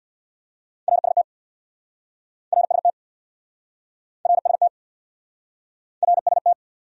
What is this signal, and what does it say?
Morse code, heard as a keyed tone, about 69 wpm, 700 Hz. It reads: RST RST RST RST